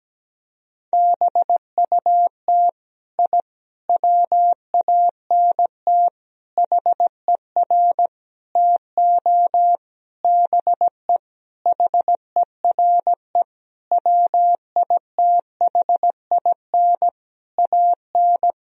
Morse code, heard as a keyed tone, about 17 wpm, 700 Hz. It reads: BUT I WANT HER TO BE HERE WITHIN AN
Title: Sentences from Top 1000 Words - Review Part 4 - 17wpm